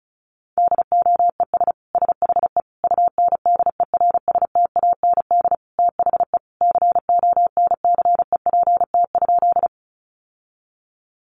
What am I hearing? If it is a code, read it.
Morse code, heard as a keyed tone, about 34 wpm, 700 Hz. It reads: DOES SHE UNDERSTAND THE CONCEPT?
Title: Sentences from Top 1100 Words 35wpm